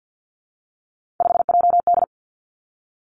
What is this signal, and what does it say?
Morse code, heard as a keyed tone, about 50 wpm, 700 Hz. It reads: 5JD
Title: Three Challenging Letter-Number Combinations 50wpm